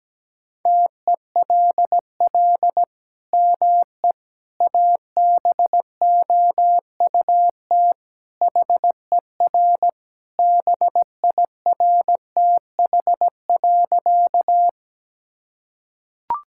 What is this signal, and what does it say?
Morse code, heard as a keyed tone, about 17 wpm, 700 Hz. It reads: TELL ME ABOUT HER BIRTH.
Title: Sentences from Top 1000 Words - Review Part 5 - 17wpm